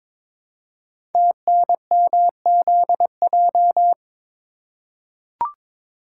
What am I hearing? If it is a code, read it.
Morse code, heard as a keyed tone, about 23 wpm, 700 Hz. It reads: TNMZJ
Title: Code Groups - 5 Letters - 22wpm